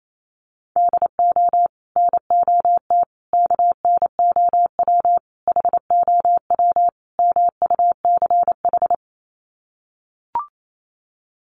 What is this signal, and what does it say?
Morse code, heard as a keyed tone, about 28 wpm, 700 Hz. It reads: DO NOT KNOW HOW MUCH
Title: Sets of 5 Words Encore 28wpm